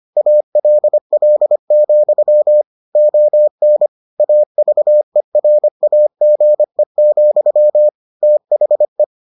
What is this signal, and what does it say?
Morse code, heard as a keyed tone, about 25 wpm, 600 Hz. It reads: ALL, ON AVERAGE, THE